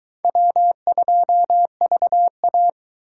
Morse code, heard as a keyed tone, about 22 wpm, 700 Hz. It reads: W2VA